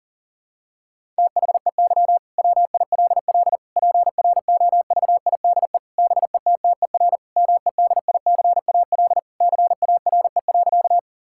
Morse code, heard as a keyed tone, about 40 wpm, 700 Hz. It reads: THEY WILL PROVIDE BETTER MEDICAL CARE.
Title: Sentences from Top 1000 Words - Review Part 5 - 40wpm